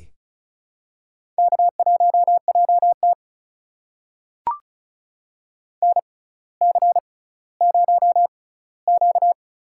Morse code, heard as a keyed tone, about 36 wpm, spaced as for 18 wpm, 700 Hz. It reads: K1JT NC0Q
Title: USA Call Signs - 4 Characters - 6x Character Spacing - 35wpm